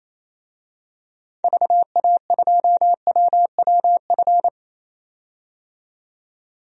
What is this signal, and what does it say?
Morse code, heard as a keyed tone, about 28 wpm, 700 Hz. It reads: VA2WWF
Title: DXCC-100 Call Signs - 6 Characters - 28wpm